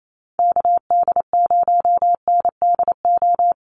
Morse code, heard as a keyed tone, about 28 wpm, 700 Hz. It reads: KD0NDO
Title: USA Call Signs Repeated 3x 28wpm